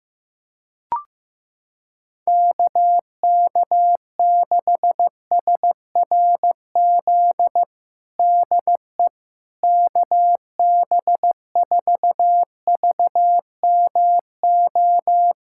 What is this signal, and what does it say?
Morse code, heard as a keyed tone, about 15 wpm, 700 Hz. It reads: KK6SRZ DE KB4VMO
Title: USA Callsign DE Callsign - 6 Characters - 15wpm